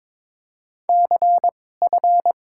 Morse code, heard as a keyed tone, about 22 wpm, spaced as for 16 wpm, 700 Hz. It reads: CF
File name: Top 90 QSO Elements - 2x Character Spacing - 22wpm